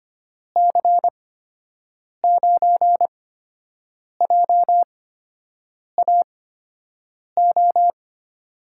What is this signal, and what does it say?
Morse code, heard as a keyed tone, about 25 wpm, spaced as for 7 wpm, 700 Hz. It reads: C9JAO
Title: USA Call Signs - 6 Characters - 8x Character Spacing - 25wpm